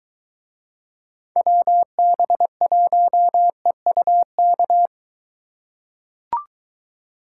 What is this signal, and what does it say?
Morse code, heard as a keyed tone, about 23 wpm, 700 Hz. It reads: WB1EUK